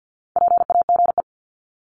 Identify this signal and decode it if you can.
Morse code, heard as a keyed tone, about 49 wpm, 700 Hz. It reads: PAGE
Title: Top 500 Words Repeated 50wpm